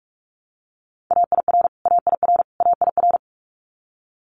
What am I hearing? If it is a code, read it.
Morse code, heard as a keyed tone, about 46 wpm, 700 Hz. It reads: AIR AIR AIR
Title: Multi-letter Phonemes Repeated 3x 45wpm